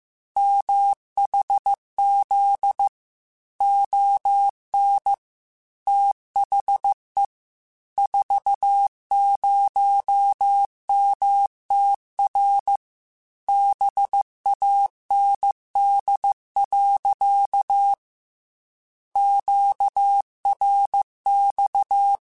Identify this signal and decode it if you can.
Morse code, heard as a keyed tone, about 15 wpm, 800 Hz. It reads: MHZ ON THE 40MTR BAND. QRX
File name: QSOs / QSOs-mp3s / WA7QOX 2@13